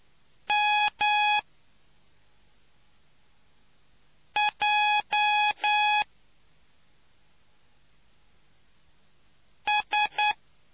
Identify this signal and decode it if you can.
Morse code, heard as a keyed tone, about 10 wpm, 840 Hz. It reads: M J S